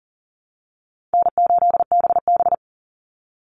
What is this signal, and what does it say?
Morse code, heard as a keyed tone, about 40 wpm, 700 Hz. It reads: N8BB
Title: NA SOTA Chaser Callsigns 40wpm